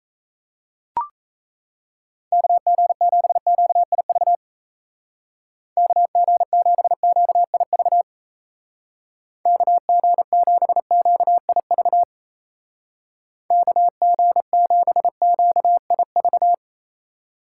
Morse code, heard as a keyed tone, about 34 wpm, 700 Hz. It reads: KG7QIV KG7QIV KG7QIV KG7QIV
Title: Speed Racing - USA 2x3 Call Signs 28wpm